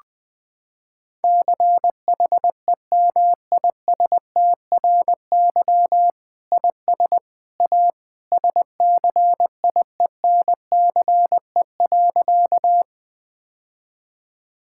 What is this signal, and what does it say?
Morse code, heard as a keyed tone, about 20 wpm, 700 Hz. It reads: CHEMISTRY IS A SCIENCE.
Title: Sentences from Top 1000 Words - Review Part 6 - 20wpm